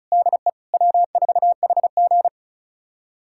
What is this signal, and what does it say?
Morse code, heard as a keyed tone, about 36 wpm, 700 Hz. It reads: DE W4HG
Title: USA Callsign DE Callsign - 4 Characters - 35wpm